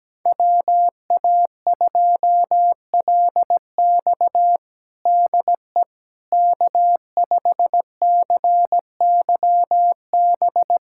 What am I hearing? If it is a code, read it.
Morse code, heard as a keyed tone, about 17 wpm, 700 Hz. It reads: WA2LX DE K5CYB
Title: USA Callsign DE Callsign - 5 Characters - 17wpm